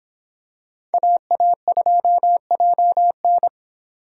Morse code, heard as a keyed tone, about 26 wpm, 700 Hz. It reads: AA2JN